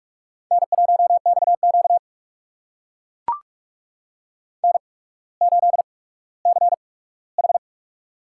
Morse code, heard as a keyed tone, about 44 wpm, spaced as for 19 wpm, 700 Hz. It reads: N1XQ N8CH
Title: USA Call Signs - 4 Characters - 8x Character Spacing - 45wpm